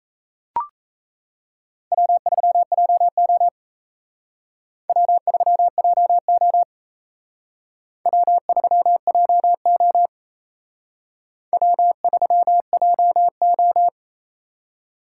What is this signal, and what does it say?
Morse code, heard as a keyed tone, about 34 wpm, 700 Hz. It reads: W3JO W3JO W3JO W3JO
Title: Speed Racing - USA 1x2 Call Signs 28wpm